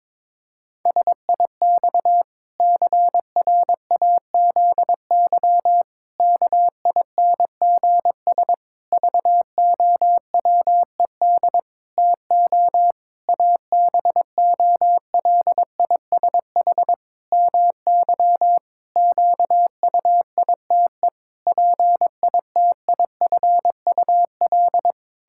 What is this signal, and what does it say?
Morse code, heard as a keyed tone, about 22 wpm, 700 Hz. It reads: SIX CRAZY KINGS VOWED TO ABOLISH MY QUITE PITIFUL
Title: Pangram Challenge v2 - 22wpm